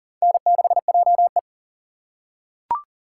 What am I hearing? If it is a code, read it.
Morse code, heard as a keyed tone, about 40 wpm, 700 Hz. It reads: N6JE
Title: USA Call Signs - 4 Characters - 2x Character Spacing - 40wpm